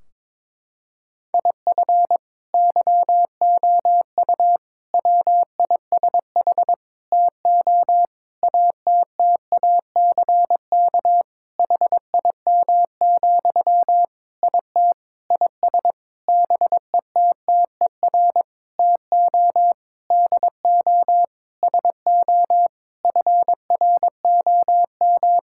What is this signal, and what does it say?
Morse code, heard as a keyed tone, about 21 wpm, 700 Hz. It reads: IF YOU WISH TO ATTACK HIM, IT IS BETTER TO DO SO FROM
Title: Sentences from Top 1600 Words 22wpm